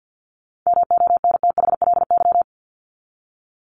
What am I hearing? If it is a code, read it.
Morse code, heard as a keyed tone, about 50 wpm, 700 Hz. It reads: MONTHLY
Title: LY Suffix - Top 100 Words with Base Words 50wpm